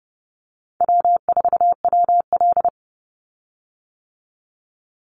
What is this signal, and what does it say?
Morse code, heard as a keyed tone, about 30 wpm, 700 Hz. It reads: W4WL
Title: USA Call Signs 30wpm